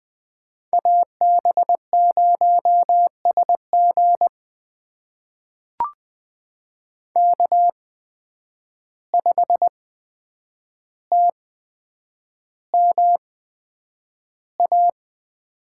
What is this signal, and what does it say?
Morse code, heard as a keyed tone, about 20 wpm, spaced as for 9 wpm, 700 Hz. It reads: AB0SG K5TMA
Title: USA Call Signs - 5 Characters - 8x Character Spacing - 20wpm